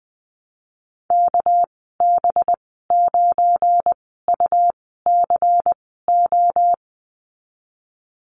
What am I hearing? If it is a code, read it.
Morse code, heard as a keyed tone, about 20 wpm, spaced as for 15 wpm, 700 Hz. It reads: KB9UCO
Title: 200 Lesson 32g - Letter k - Call Signs 20wpm